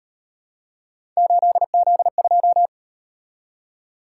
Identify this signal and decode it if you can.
Morse code, heard as a keyed tone, about 38 wpm, 700 Hz. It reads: 8Z2